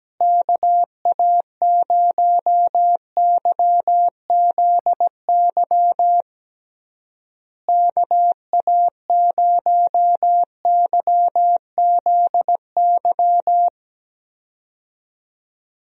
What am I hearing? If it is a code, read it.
Morse code, heard as a keyed tone, about 17 wpm, 700 Hz. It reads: KA0YZY KA0YZY
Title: USA Call Signs - 6 Characters - Repeated 3x - 17wpm